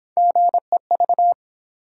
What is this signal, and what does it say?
Morse code, heard as a keyed tone, about 29 wpm, 700 Hz. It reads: GEV